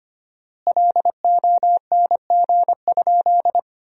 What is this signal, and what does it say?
Morse code, heard as a keyed tone, about 25 wpm, 700 Hz. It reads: LONG?
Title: 4x Word-Spacing - Sentences from Top 200 Words - 25wpm